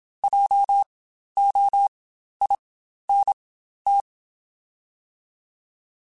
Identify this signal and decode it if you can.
Morse code, heard as a keyed tone, about 27 wpm, spaced as for 12 wpm, 800 Hz. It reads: JOINT
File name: Spaced Alphas / Spaced Alphas mp3s / Js sp @25